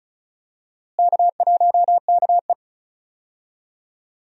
Morse code, heard as a keyed tone, about 34 wpm, 700 Hz. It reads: K1KE
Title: USA Call Signs - 4 Characters - Repeated 3x - 35wpm